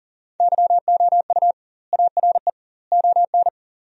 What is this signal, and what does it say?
Morse code, heard as a keyed tone, about 40 wpm, 700 Hz. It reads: YOU ARE ON
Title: 2x Word-Spacing - Sentences from Top 400 Words - 40wpm